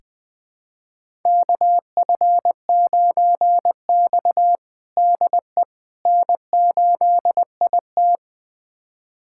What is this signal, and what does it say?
Morse code, heard as a keyed tone, about 20 wpm, 700 Hz. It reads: KF9X DE N8IT
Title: USA Callsign DE Callsign - 4 Characters - 20wpm